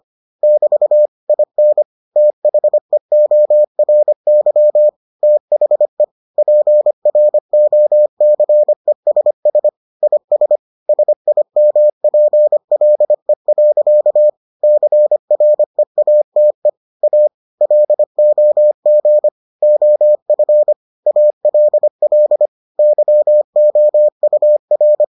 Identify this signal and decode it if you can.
Morse code, heard as a keyed tone, about 25 wpm, 600 Hz. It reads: <BT> IN THEORY THE PROCESS IS SIMPLE. CREATE A LOG OF ALL YOUR